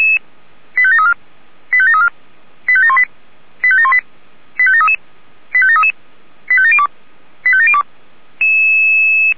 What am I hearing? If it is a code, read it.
Five-tone selective call (ZVEI), heard as quick runs of five tones, tones of about 70 ms. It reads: 85734 85734 85718 85718 85733 85733 85702 85702
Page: ZVEI